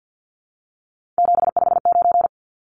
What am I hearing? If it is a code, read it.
Morse code, heard as a keyed tone, about 50 wpm, 700 Hz. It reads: 759
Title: Numbers - 3-Digit 50wpm